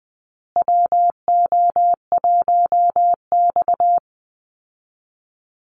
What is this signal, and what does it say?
Morse code, heard as a keyed tone, about 20 wpm, 700 Hz. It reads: WO1X
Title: Call Signs Repeated 2x - 20wpm